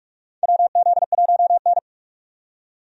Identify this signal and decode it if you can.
Morse code, heard as a keyed tone, about 44 wpm, 700 Hz. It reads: WZ1N
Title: USA Call Signs - 4 Characters - 45wpm